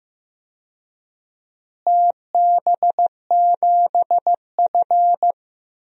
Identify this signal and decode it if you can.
Morse code, heard as a keyed tone, about 15 wpm, 700 Hz. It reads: TB7F
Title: DXCC-100 Call Signs - 4 Characters - Repeated 2x - 15wpm